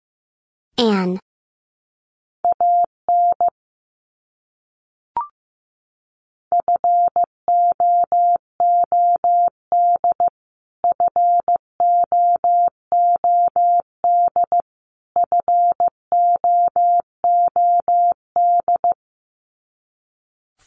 Morse code, heard as a keyed tone, about 15 wpm, 700 Hz. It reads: AN FOOD FOOD FOOD
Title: Top 300 Words Repeated 15wpm